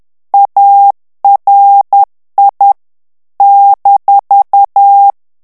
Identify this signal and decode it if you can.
Morse code, heard as a keyed tone, about 10 wpm, 800 Hz. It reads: ARI -
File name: ITALIEN texte clair 2 vitesse 10 mots minute